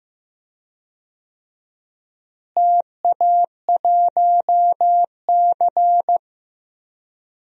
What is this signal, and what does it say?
Morse code, heard as a keyed tone, about 15 wpm, 700 Hz. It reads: TA1C
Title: DXCC-100 Call Signs - 4 Characters - Repeated 3x - 15wpm